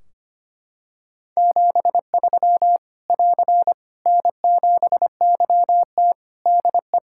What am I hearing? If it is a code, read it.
Morse code, heard as a keyed tone, about 25 wpm, 700 Hz. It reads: 73 <AR> N7YT DE
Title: Practice QSO - Exchange 4 - Ending - 25wpm